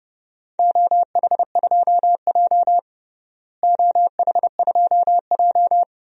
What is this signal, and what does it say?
Morse code, heard as a keyed tone, about 30 wpm, 700 Hz. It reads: OH2J OH2J
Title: DXCC-100 Call Signs - 4 Characters - Repeated 2x - 30wpm